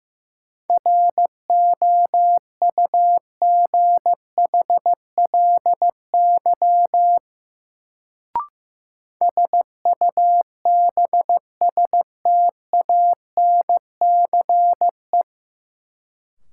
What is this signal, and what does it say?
Morse code, heard as a keyed tone, about 15 wpm, 700 Hz. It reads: ROUGHLY SUBSTANCE
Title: Sentences from Top 1500 Words - New Word Drill - 15wpm